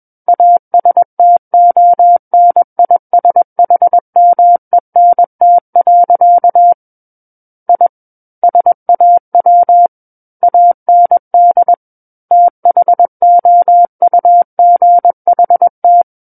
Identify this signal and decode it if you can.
Morse code, heard as a keyed tone, about 21 wpm, 700 Hz. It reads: ASTONISHMENT. I SAW AND THOUGHT
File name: Book One / 12-Chapter 12 - 21 WPM